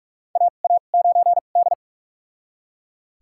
Morse code, heard as a keyed tone, about 41 wpm, spaced as for 31 wpm, 700 Hz. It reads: AA9D
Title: USA Call Signs - 2x Character Spacing - 45wpm